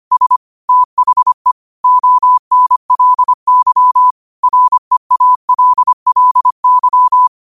Morse code, heard as a keyed tone, about 25 wpm, 1000 Hz. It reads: S THE ONLY REALLY
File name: qotdfiles / 25wpm